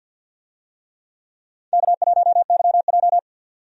Morse code, heard as a keyed tone, about 51 wpm, 700 Hz. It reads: K1YJ
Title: USA Call Signs - 4 Characters - 50wpm